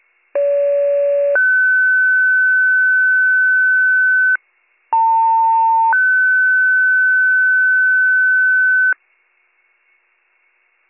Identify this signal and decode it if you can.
Two-tone page, heard as a one-second tone, then a three-second tone, about 570 then 1500 Hz, about 900 then 1500 Hz. It is heard from 0.3 to 8.9 s.